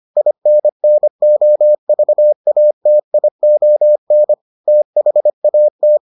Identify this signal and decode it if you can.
Morse code, heard as a keyed tone, about 25 wpm, 600 Hz. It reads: INNOVATION THAT